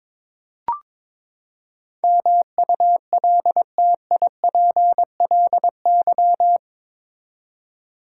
Morse code, heard as a keyed tone, about 22 wpm, 700 Hz. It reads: MULTIPLY
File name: practice / 22wpm / Eight Letter Words - 22wpm